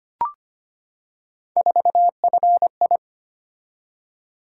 Morse code, heard as a keyed tone, about 25 wpm, 700 Hz. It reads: 4FI
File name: Three Letter-Number Combinations - 25wpm